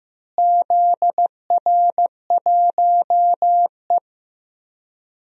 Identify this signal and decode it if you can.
Morse code, heard as a keyed tone, about 15 wpm, 700 Hz. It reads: ZR1E
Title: DXCC-100 Call Signs - 4 Characters - Repeated 3x - 15wpm